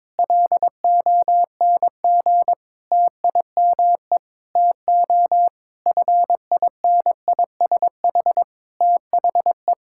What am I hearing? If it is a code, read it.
Morse code, heard as a keyed tone, about 22 wpm, 700 Hz. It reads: LONG TIME TO FINISH THE
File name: Sentences with 7-Word Phrases - 22wpm